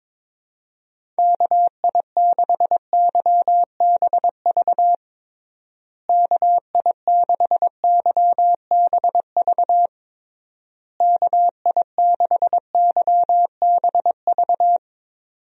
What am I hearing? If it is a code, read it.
Morse code, heard as a keyed tone, about 22 wpm, 700 Hz. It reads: KI6YBV KI6YBV KI6YBV
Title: USA Call Signs - 6 Characters - Repeated 3x - 22wpm